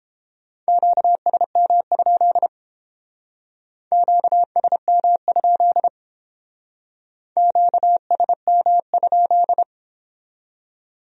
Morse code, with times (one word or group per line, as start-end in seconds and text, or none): 0.7-2.5 QSM?
3.9-5.9 QSM?
7.4-9.6 QSM?